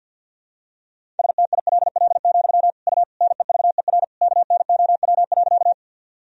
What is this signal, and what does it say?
Morse code, heard as a keyed tone, about 52 wpm, 700 Hz. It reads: STILL, U NEVER KNOW.
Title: Sentences from Top 300 Words and Abbreviations 50wpm